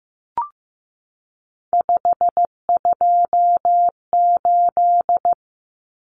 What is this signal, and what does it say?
Morse code, heard as a keyed tone, about 15 wpm, 700 Hz. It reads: 528
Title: Common Cut Numbers - 3-Digit 15wpm